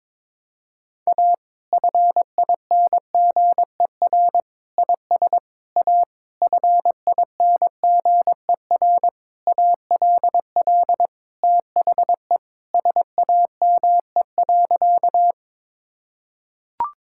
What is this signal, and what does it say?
Morse code, heard as a keyed tone, about 21 wpm, 700 Hz. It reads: A FINGER IS A FINGER ALL THE SAME.
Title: Sentences from Top 1600 Words 22wpm